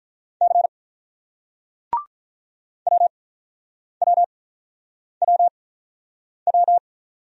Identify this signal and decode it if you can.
Morse code, heard as a keyed tone, about 48 wpm, spaced as for 9 wpm, 700 Hz. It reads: C WWWW